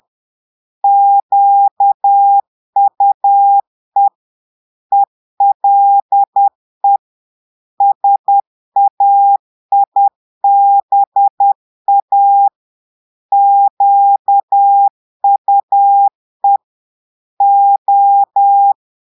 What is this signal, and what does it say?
Morse code, heard as a keyed tone, about 10 wpm, 800 Hz. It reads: QUE ELE SAIBA QUE O
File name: portugais texte en clair 2 vitesse 10 mots minutes